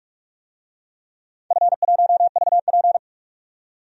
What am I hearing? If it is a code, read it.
Morse code, heard as a keyed tone, about 46 wpm, 700 Hz. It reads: F1VP
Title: DXCC-100 Call Signs - 4 Characters - 45wpm